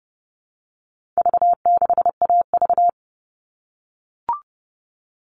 Morse code, heard as a keyed tone, about 30 wpm, 700 Hz. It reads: V6AV
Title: All Cut Numbers - 4-Digit 30wpm